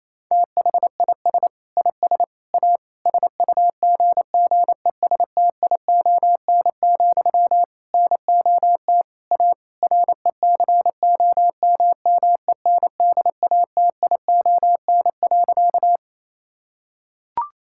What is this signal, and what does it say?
Morse code, heard as a keyed tone, about 28 wpm, 700 Hz. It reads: THIS IS A SUGGESTION, NOT A RECOMMENDATION.
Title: Sentences from Top 1600 Words 28wpm